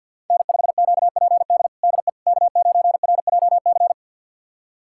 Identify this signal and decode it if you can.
Morse code, heard as a keyed tone, about 52 wpm, 700 Hz. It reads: N5QPD DE K9RJC
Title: USA Callsign DE Callsign - 5 Characters - 50wpm